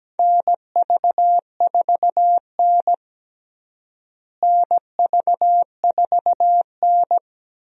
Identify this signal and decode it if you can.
Morse code, heard as a keyed tone, about 17 wpm, 700 Hz. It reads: NV4N NV4N
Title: USA Call Signs - 4 Characters - Repeated 2x - 17wpm